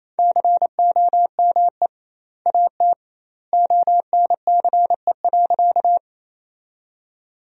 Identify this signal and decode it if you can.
Morse code, heard as a keyed tone, about 27 wpm, 700 Hz. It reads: COME AT ONCE.
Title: 2x Word-Spacing - Sentences from Top 300 Words - 28wpm